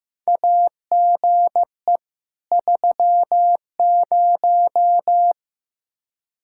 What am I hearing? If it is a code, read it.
Morse code, heard as a keyed tone, about 15 wpm, 700 Hz. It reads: AGE 30
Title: Practice QSO - Exchange 3 - More Info - 15wpm